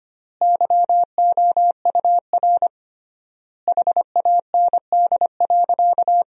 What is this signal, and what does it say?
Morse code, heard as a keyed tone, about 25 wpm, 700 Hz. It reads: YOUR HAND.
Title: 3x Word-Spacing - Sentences from Top 400 Words - 25wpm